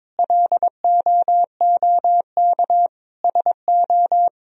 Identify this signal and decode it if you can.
Morse code, heard as a keyed tone, about 22 wpm, 700 Hz. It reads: LOOK SO
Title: Sentences from Top 1000 Words - Review Part 1 - 22wpm